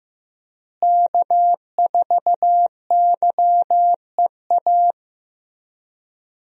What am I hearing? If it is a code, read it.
Morse code, heard as a keyed tone, about 15 wpm, 700 Hz. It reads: K4YEA
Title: USA Call Signs - 5 Characters - Repeated 3x - 15wpm